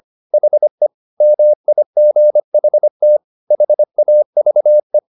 Morse code, heard as a keyed tone, about 25 wpm, 600 Hz. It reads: HE MIGHT HAVE
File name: the old man and the sea part 1 25wpm